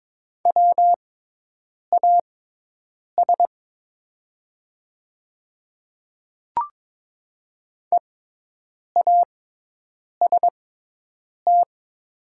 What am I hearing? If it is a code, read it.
Morse code, heard as a keyed tone, about 22 wpm, spaced as for 8 wpm, 700 Hz. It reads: WAS EAST
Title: 6x Character Spacing - Top 500 Words - 22wpm